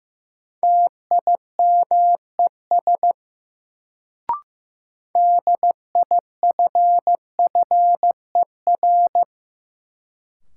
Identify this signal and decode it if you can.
Morse code, heard as a keyed tone, about 15 wpm, 700 Hz. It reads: TIMES DIFFER
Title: Sentences from Top 1100 Words - New Word Drill Alt - 15wpm